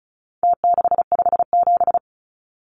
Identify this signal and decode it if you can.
Morse code, heard as a keyed tone, about 34 wpm, 700 Hz. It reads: T657
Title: Numbers - Contest Style - 4-Digit 35wpm